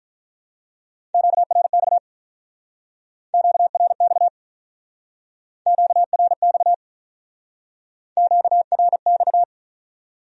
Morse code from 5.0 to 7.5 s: QRX

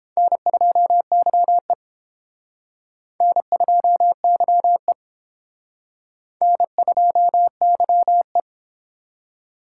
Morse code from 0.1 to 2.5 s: N2YE

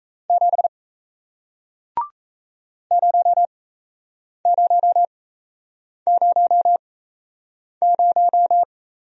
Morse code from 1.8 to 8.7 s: 0000

